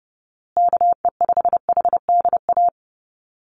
Morse code, heard as a keyed tone, about 30 wpm, 700 Hz. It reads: KE5HDA
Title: USA Call Signs Repeated 2x 30wpm